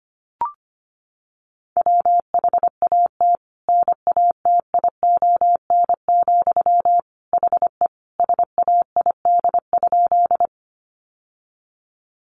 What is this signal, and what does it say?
Morse code, heard as a keyed tone, about 25 wpm, 700 Hz. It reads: WHAT NATION, HE SAID?
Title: Sentences from Top 900 Words 25wpm